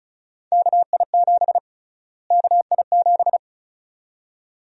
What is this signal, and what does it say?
Morse code, heard as a keyed tone, about 36 wpm, 700 Hz. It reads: KI7 KI7
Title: USA Call Sign Prefixes - Repeated 2x - 35wpm